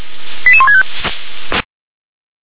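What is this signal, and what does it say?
Five-tone selective call (ZVEI), heard as quick runs of five tones, tones of about 70 ms. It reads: 88165
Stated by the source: ZVEI five-tone call received on SDR.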